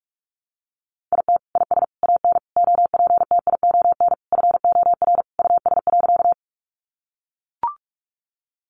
Morse code, heard as a keyed tone, about 44 wpm, 700 Hz. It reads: IT IS AN OPTION FOR US.
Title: Sentences from Top 1300 Words 45wpm